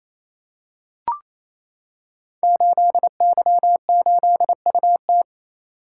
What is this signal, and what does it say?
Morse code, heard as a keyed tone, about 28 wpm, 700 Hz. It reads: 8Y8UT
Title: Code Groups - 5 Letter:Numbers - 28wpm